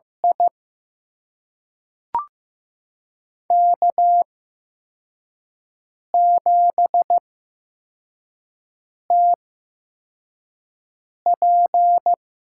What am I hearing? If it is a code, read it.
Morse code, heard as a keyed tone, about 15 wpm, spaced as for 4 wpm, 700 Hz. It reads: I K7TP
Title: USA Call Signs - 4 Characters - 8x Character Spacing - 15wpm